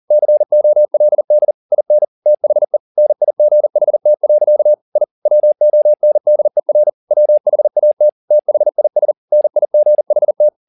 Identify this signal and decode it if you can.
Morse code, heard as a keyed tone, about 40 wpm, 600 Hz. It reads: COLD IN THE NIGHT. I WONDER WHAT THIS NIGHT
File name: the old man and the sea 2 40wpm